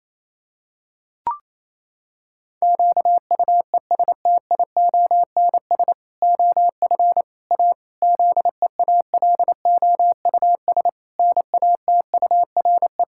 Morse code, heard as a keyed tone, about 28 wpm, 700 Hz. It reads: QUESTIONS OF A ZEALOUS NATURE
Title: Pangram Challenge v2 - 28wpm